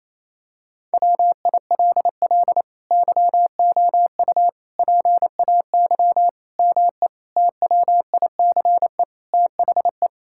Morse code, heard as a keyed tone, about 28 wpm, 700 Hz. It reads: WILL YOU PAY ME TWICE THE
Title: Sentences from Top 1600 Words 28wpm